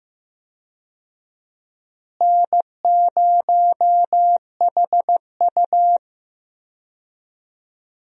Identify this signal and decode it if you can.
Morse code, heard as a keyed tone, about 15 wpm, 700 Hz. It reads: N0HU
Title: USA Call Signs - 4 Characters - Repeated 3x - 15wpm